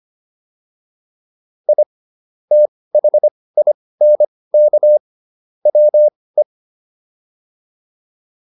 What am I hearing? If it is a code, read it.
Morse code, heard as a keyed tone, about 25 wpm, spaced as for 18 wpm, 600 Hz. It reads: I THINK WE